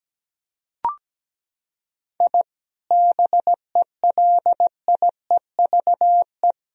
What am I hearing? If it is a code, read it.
Morse code, heard as a keyed tone, about 17 wpm, 700 Hz. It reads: I BELIEVE
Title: Sentences from Top 1000 Words - Review Part 3 - 17wpm